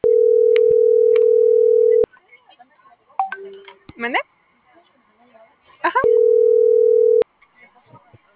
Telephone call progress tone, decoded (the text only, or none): ringback tone